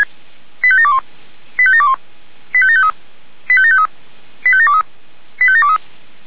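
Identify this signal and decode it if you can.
Five-tone selective call (ZVEI), heard as quick runs of five tones, tones of about 70 ms. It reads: 85721 85721 85753 85753 85723 85723
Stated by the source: ZVEI five-tone selective call received on an SDR.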